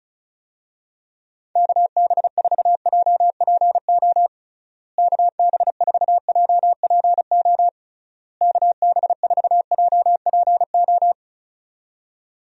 Morse code, heard as a keyed tone, about 36 wpm, 700 Hz. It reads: KB4JPO KB4JPO KB4JPO
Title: USA Call Signs - 6 Characters - Repeated 3x - 35wpm